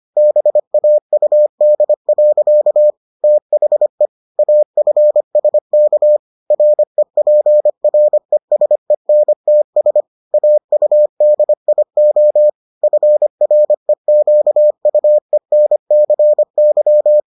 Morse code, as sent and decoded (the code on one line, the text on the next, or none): -... .- ..- -.. .-.-.- / - .... / .- ..-. ... -.- / .-. .--. .-. ... -. - ... / .- ..- -.. .. --- / ..-. .-. --.- ..- . -. -.-. -.--
BAUD. THE AFSK REPRESENTS AUDIO FREQUENCY